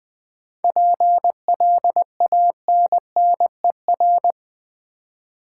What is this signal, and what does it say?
Morse code, heard as a keyed tone, about 20 wpm, 700 Hz. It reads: PLANNER
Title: Difficult Words with Hints - 20wpm